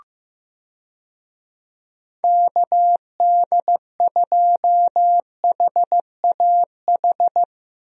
Morse code, heard as a keyed tone, about 15 wpm, 700 Hz. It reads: KD2HAH
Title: USA Call Signs - 6 Characters - Repeated 3x - 15wpm